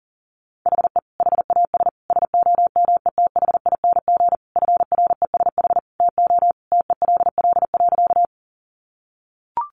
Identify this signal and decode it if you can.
Morse code, heard as a keyed tone, about 37 wpm, 700 Hz. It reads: HE HAS SOMETHING FRESH TO TELL.
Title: Sentences from Top 1200 Words 40wpm